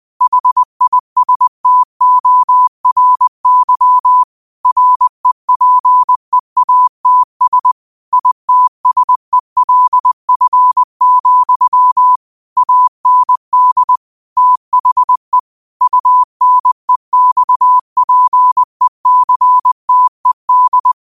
Morse code, heard as a keyed tone, about 20 wpm, 1000 Hz. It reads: HISTORY REPEATS ITSELF, AND THE UNEXPECTED
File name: qotdfiles / 20wpm